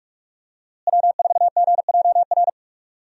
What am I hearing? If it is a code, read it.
Morse code, heard as a keyed tone, about 46 wpm, 700 Hz. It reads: W4GJR
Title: USA Call Signs - 5 Characters - Repeated 3x - 45wpm